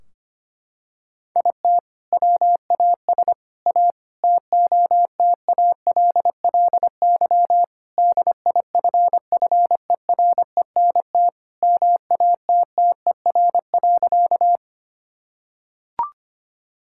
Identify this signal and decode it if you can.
Morse code, heard as a keyed tone, about 25 wpm, 700 Hz. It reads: IT WAS A TOTALLY DIFFERENT MATTER.
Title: Sentences from Top 1600 Words 25wpm